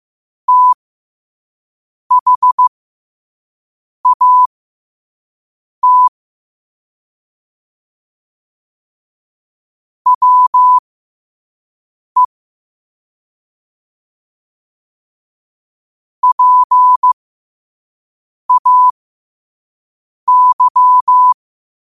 Morse code, heard as a keyed tone, about 15 wpm, spaced as for 5 wpm, 1000 Hz. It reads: THAT WE PAY
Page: Quotes for Wed, 13 Aug 2025 in Morse Code at 5 words per minute.